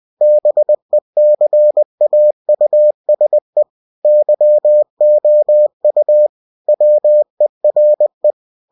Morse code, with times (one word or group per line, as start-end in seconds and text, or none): 0.2-3.6 BECAUSE
4.0-6.3 YOU
6.7-8.3 WERE